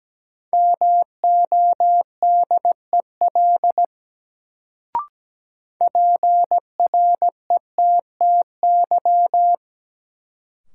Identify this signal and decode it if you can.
Morse code, heard as a keyed tone, about 17 wpm, 700 Hz. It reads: MODEL PRETTY
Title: Sentences from Top 1000 Words - New Word Drill - 17wpm